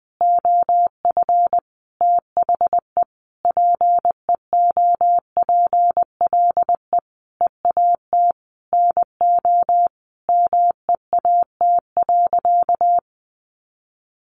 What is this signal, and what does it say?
Morse code, heard as a keyed tone, about 20 wpm, 700 Hz. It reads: OF THE PEOPLE EAT NO MEAT.
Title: Sentences from Top 1300 Words 20wpm